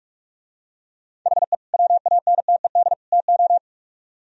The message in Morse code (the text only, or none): HE WANTED TO